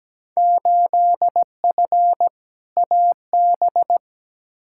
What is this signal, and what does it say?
Morse code, heard as a keyed tone, about 17 wpm, 700 Hz. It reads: 8F AB
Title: Field Day Exchanges Encore - 17wpm